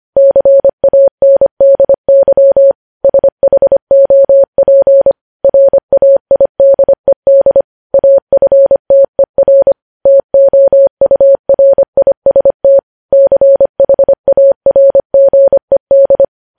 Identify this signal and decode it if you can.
Morse code, heard as a keyed tone, about 25 wpm, 550 Hz. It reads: CANDY SHOP RAIDED AFTER TOURIST CHARGED
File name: podcast / News headlines-1x-25x25